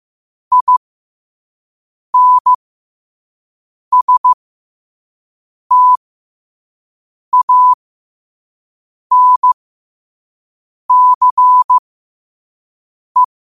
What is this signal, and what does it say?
Morse code, heard as a keyed tone, about 15 wpm, spaced as for 5 wpm, 1000 Hz. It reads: INSTANCE
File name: qotdfiles / 5wpm